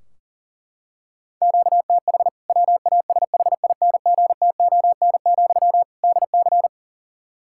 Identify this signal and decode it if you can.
Morse code, heard as a keyed tone, about 37 wpm, 700 Hz. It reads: QTH WASHINGTON, DC
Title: K1USN SST QSO Practice 40wpm